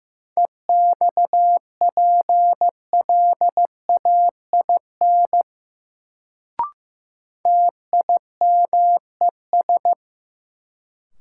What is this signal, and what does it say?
Morse code, heard as a keyed tone, about 15 wpm, 700 Hz. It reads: EXPLAIN TIMES
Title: Sentences from Top 1100 Words - New Word Drill Alt - 15wpm